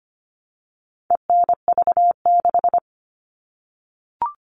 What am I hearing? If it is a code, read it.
Morse code, heard as a keyed tone, about 25 wpm, 700 Hz. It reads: ENV6
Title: All Cut Numbers - 4-Digit 25wpm